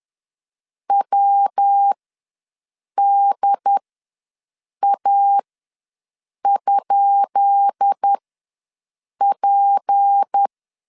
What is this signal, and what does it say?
Morse code, heard as a keyed tone, about 10 wpm, spaced as for 6 wpm, 800 Hz. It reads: WDA?P
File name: LEZIONI / 10 SIGNS / 10B INT ws 10wpm 4134